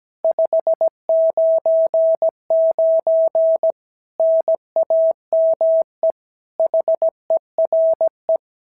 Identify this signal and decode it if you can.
Morse code, heard as a keyed tone, about 16 wpm, 650 Hz. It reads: 599 NAME HERE
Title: SOTA QSO Practice 17wpm